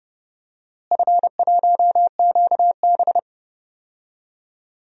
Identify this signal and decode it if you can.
Morse code, heard as a keyed tone, about 30 wpm, 700 Hz. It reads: F1QB
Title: DXCC-100 Call Signs - 4 Characters - Repeated 2x - 30wpm